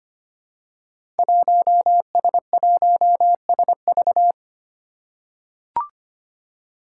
Morse code, heard as a keyed tone, about 25 wpm, 700 Hz. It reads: 1S1SV
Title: Five Challenging Letter-Number Combinations 25wpm